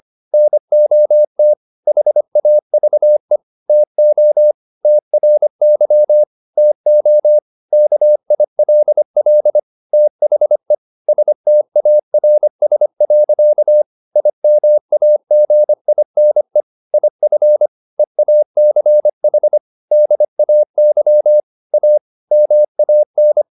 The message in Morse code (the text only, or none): NOT HAVE TO TRY TO KILL THE STARS. IMAGINE IF EACH DAY A MAN